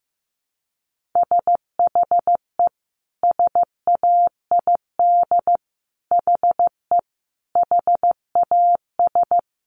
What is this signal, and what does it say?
Morse code, heard as a keyed tone, about 15 wpm, 700 Hz. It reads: SHE SAID HE HAS
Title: Sentences from Top 1300 Words 15wpm